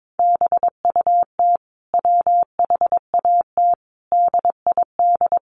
Morse code, heard as a keyed tone, about 22 wpm, 700 Hz. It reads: BUT WHAT DID
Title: Sentences from Top 400 Words 22wpm